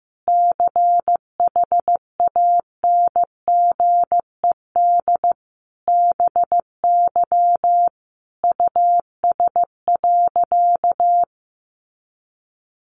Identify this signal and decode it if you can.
Morse code, heard as a keyed tone, about 15 wpm, 700 Hz. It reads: CHANGED BY US.
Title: A Sound Of Thunder 15wpm